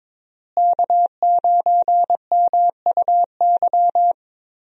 Morse code, heard as a keyed tone, about 21 wpm, 700 Hz. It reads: K9MUY
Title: USA Call Signs - 5 Characters - Repeated 3x - 22wpm